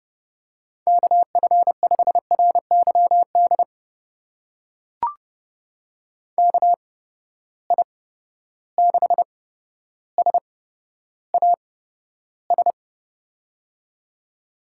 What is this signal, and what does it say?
Morse code, heard as a keyed tone, about 31 wpm, spaced as for 13 wpm, 700 Hz. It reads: KF5RYD KI6SAS
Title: USA Call Signs - 6 Characters - 8x Character Spacing - 30wpm